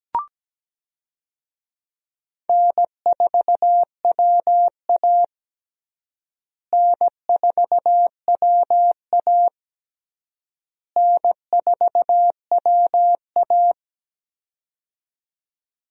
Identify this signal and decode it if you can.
Morse code, heard as a keyed tone, about 17 wpm, 700 Hz. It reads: N4WA N4WA N4WA
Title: USA Call Signs - 4 Characters - Repeated 3x - 17wpm